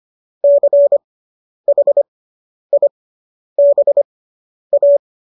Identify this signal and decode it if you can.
Morse code, heard as a keyed tone, about 25 wpm, spaced as for 10 wpm, 570 Hz. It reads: CHIBA
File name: qso005 10